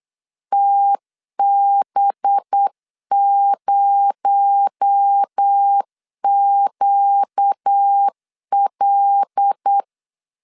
Morse code, heard as a keyed tone, about 8 wpm, 800 Hz. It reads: TB0QL
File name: 8wpm